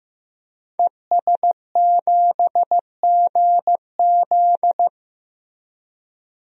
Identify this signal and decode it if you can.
Morse code, heard as a keyed tone, about 15 wpm, 700 Hz. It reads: ES7GZ
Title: DXCC-100 Call Signs - 5 Characters - Repeated 2x - 15wpm